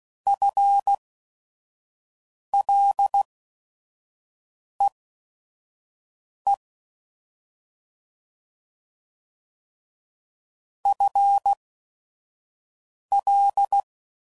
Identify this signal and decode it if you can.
Morse code, heard as a keyed tone, about 16 wpm, spaced as for 5 wpm, 800 Hz. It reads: FLEE FL